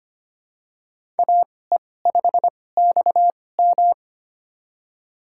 Morse code, heard as a keyed tone, about 25 wpm, spaced as for 18 wpm, 700 Hz. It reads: AE5XM
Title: USA Call Signs - 2x Character Spacing - 25wpm